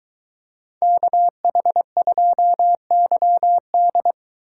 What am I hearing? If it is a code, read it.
Morse code, heard as a keyed tone, about 23 wpm, 700 Hz. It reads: KH2YD